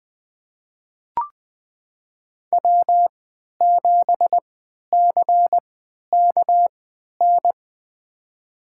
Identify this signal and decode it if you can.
Morse code, heard as a keyed tone, about 20 wpm, spaced as for 11 wpm, 700 Hz. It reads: W7CKN
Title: USA Call Signs - 5 Characters - 3x Character Spacing - 20wpm